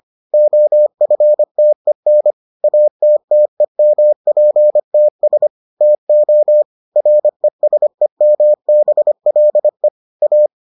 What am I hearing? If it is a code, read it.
Morse code, heard as a keyed tone, about 25 wpm, 600 Hz. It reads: OFTEN ATTEMPTS TO RESEMBLE A